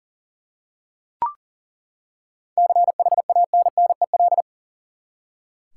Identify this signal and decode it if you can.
Morse code, heard as a keyed tone, about 38 wpm, 700 Hz. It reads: CHANNEL
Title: Seven Letter Words - 40wpm